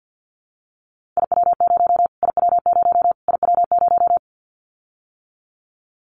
Morse code, heard as a keyed tone, about 50 wpm, 700 Hz. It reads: IW0 IW0 IW0
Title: Callsign Prefixes for Top 25 DXCC - Repeated 3x - 50wpm